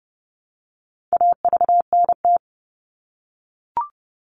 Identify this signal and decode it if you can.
Morse code, heard as a keyed tone, about 30 wpm, 700 Hz. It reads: AVNT
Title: All Cut Numbers - 4-Digit 30wpm